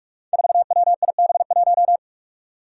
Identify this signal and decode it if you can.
Morse code, heard as a keyed tone, about 44 wpm, 700 Hz. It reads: 4WIB1
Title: Five Challenging Letter-Number Combinations 45wpm